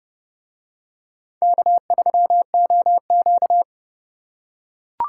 Morse code, heard as a keyed tone, about 29 wpm, 700 Hz. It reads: K3OQ